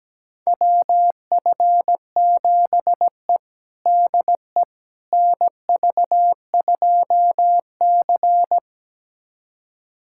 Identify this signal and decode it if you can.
Morse code, heard as a keyed tone, about 17 wpm, 700 Hz. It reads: WF7E DE NV2C
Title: USA Callsign DE Callsign - 4 Characters - 17wpm